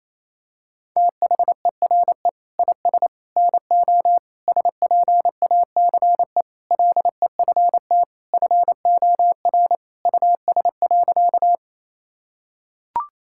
Morse code, heard as a keyed tone, about 28 wpm, 700 Hz. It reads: THERE IS NO SPACE LEFT FOR US.
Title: Sentences from Top 1000 Words - Review Part 5 - 28wpm